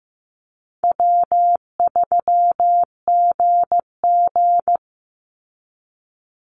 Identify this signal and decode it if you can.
Morse code, heard as a keyed tone, about 15 wpm, 700 Hz. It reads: W3GG
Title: Easy Call Signs Repeated 3x - 15wpm